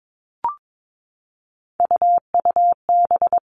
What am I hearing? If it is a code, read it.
Morse code, heard as a keyed tone, about 22 wpm, 700 Hz. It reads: UUB
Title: Three Challenging Letter-Number Combinations 22wpm